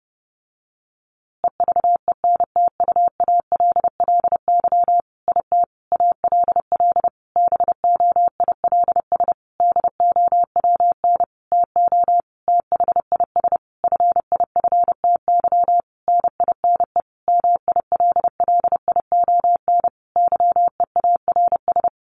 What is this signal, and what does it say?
Morse code, heard as a keyed tone, about 30 wpm, 700 Hz. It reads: EVENTUALLY IT ALL BOILS DOWN TO THIS FIFTY NINE MILLION YEARS
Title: A Sound Of Thunder 30wpm